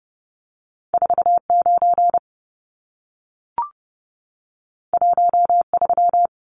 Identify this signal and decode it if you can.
Morse code, heard as a keyed tone, about 30 wpm, 700 Hz. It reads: 49 13
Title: Numbers - 2-Digit 30wpm